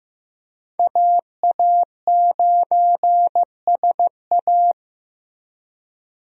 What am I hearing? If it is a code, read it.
Morse code, heard as a keyed tone, about 15 wpm, 700 Hz. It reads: AA9SA
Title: USA Call Signs - 5 Characters - Repeated 2x - 15wpm